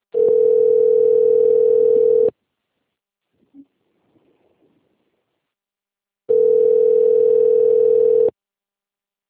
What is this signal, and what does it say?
Ringback tone (North America), 440 + 480 Hz.